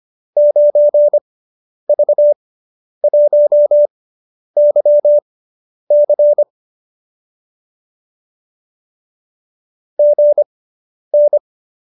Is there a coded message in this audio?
Yes: Morse code from 0.4 to 11.4 s, heard as a keyed tone, about 25 wpm, spaced as for 10 wpm, 590 Hz. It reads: 9V1YC GN